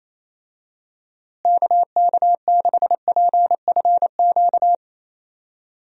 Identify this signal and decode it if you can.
Morse code, heard as a keyed tone, about 28 wpm, 700 Hz. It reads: KK6PFQ
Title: USA Call Signs - 6 Characters - Repeated 3x - 28wpm